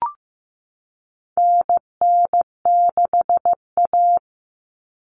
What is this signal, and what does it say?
Morse code, heard as a keyed tone, about 15 wpm, 700 Hz. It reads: NN6A
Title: Common Cut Numbers - 4-Digit 15wpm